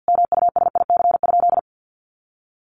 Morse code, heard as a keyed tone, about 50 wpm, 700 Hz. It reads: MUSIC?
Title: Sentences from Top 300 Words 50wpm